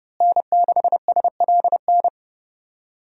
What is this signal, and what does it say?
Morse code, heard as a keyed tone, about 30 wpm, 700 Hz. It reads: N6SLN
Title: USA Call Signs - 5 Characters - Repeated 2x - 30wpm